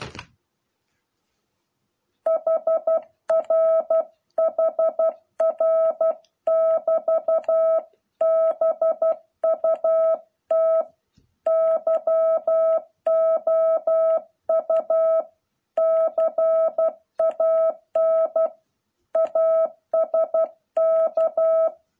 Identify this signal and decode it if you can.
Morse code, heard as a keyed tone, about 12 wpm, 660 Hz. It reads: HR HR <BT> BUT YOU CAN ASK